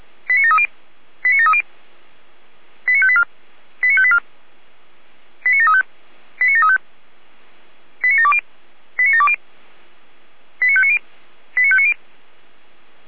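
Five-tone selective call (ZVEI), heard as quick runs of five tones, tones of about 70 ms. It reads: 89730 89730 89584 89584 89735 89735 89720 89720 89590 89590
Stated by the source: Five-tone ZVEI selective call, code 89752, received by SDR.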